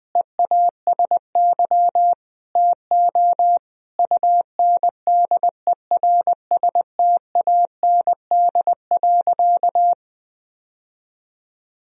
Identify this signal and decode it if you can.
Morse code, heard as a keyed tone, about 21 wpm, 700 Hz. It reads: EASY TO UNDERSTAND.
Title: Sentences from Top 1000 Words - Review Part 3 - 20wpm